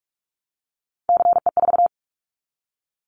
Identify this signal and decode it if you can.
Morse code, heard as a keyed tone, about 44 wpm, 700 Hz. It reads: CE4
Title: Callsign Prefixes for Top 100 DXCC - 45wpm